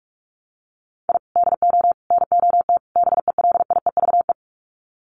Morse code, heard as a keyed tone, about 44 wpm, 700 Hz. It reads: I DO NOT BELIEVE
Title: Sets of 4 Words 45wpm